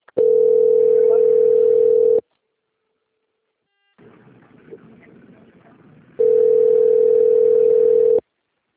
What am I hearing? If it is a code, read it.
Ringback tone (North America), 440 + 480 Hz.